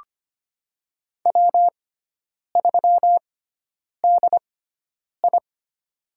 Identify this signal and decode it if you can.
Morse code, heard as a keyed tone, about 25 wpm, spaced as for 9 wpm, 700 Hz. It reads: W3DI
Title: USA Call Signs - 4 Characters - 6x Character Spacing - 25wpm